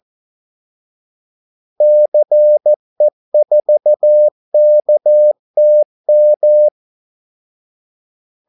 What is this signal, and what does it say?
Morse code, heard as a keyed tone, about 14 wpm, 600 Hz. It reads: CE4KTM
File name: Callsigns-14wpm